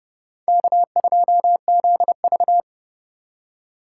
Morse code, heard as a keyed tone, about 35 wpm, 700 Hz. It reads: K2ZV
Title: Speed Racing - USA 1x2 Call Signs 30wpm